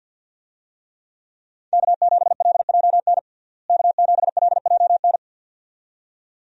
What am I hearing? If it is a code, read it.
Morse code, heard as a keyed tone, about 51 wpm, 700 Hz. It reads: K7LJN K7LJN
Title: USA Call Signs - 5 Characters - Repeated 2x - 50wpm